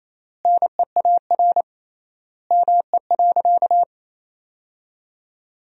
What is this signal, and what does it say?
Morse code, heard as a keyed tone, about 27 wpm, 700 Hz. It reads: NEAR ME.
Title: 3x Word-Spacing - Sentences from Top 300 Words - 28wpm